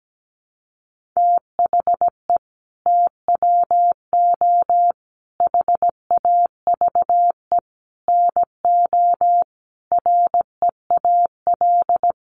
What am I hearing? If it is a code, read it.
Morse code, heard as a keyed tone, about 17 wpm, 700 Hz. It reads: THE TWO HAVE NO REAL
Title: Sentences from Top 1300 Words 17wpm